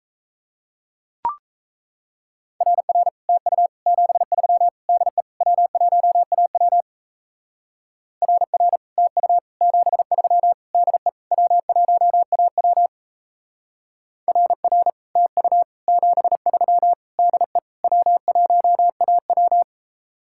Morse code, heard as a keyed tone, about 34 wpm, 700 Hz. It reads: RR TU 73 DE W1AW RR TU 73 DE W1AW RR TU 73 DE W1AW
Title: Speed-Racing Common QSO Phrases - 28wpm